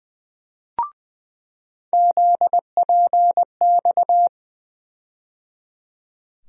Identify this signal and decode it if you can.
Morse code, heard as a keyed tone, about 20 wpm, 700 Hz. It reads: ZPX
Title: USA Call Sign Postfixes - 3 Letters 20wpm